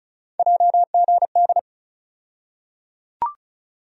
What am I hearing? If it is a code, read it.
Morse code, heard as a keyed tone, about 34 wpm, 700 Hz. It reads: JGD
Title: USA Call Sign Postfixes - 3 Letters 35wpm